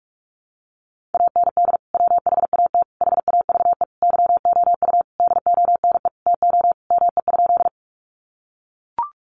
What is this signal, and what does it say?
Morse code, heard as a keyed tone, about 44 wpm, 700 Hz. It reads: AND WHAT HAVE YOU DONE TO ME?
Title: Sentences from Top 500 Words 45wpm